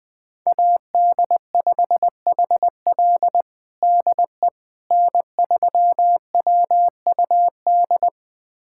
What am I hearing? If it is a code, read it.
Morse code, heard as a keyed tone, about 20 wpm, 700 Hz. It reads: AD5HL DE N3WUD
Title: USA Callsign DE Callsign - 5 Characters - 20wpm